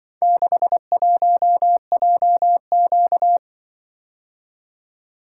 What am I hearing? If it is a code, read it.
Morse code, heard as a keyed tone, about 23 wpm, 700 Hz. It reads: 61JQ
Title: Speed-Racing - Four Letter-Number Combinations - 20wpm